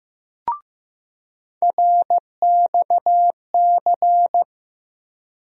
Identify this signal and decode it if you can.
Morse code, heard as a keyed tone, about 15 wpm, 700 Hz. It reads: RXC
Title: USA Call Sign Postfixes - 3 Letters 15wpm